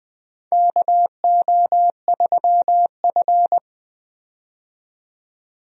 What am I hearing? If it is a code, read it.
Morse code, heard as a keyed tone, about 20 wpm, 700 Hz. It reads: KO3F
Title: USA Call Signs - 4 Characters - 20wpm